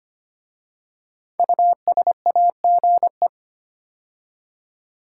Morse code, heard as a keyed tone, about 25 wpm, 700 Hz. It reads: USAGE
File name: 5 Letter Words - First 3 Letters Given - 25wpm